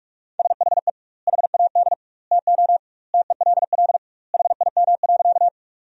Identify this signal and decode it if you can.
Morse code, heard as a keyed tone, about 42 wpm, 700 Hz. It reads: SHE HAD TO TELL HIM.
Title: 2x Word-Spacing - Sentences from Top 200 Words - 45wpm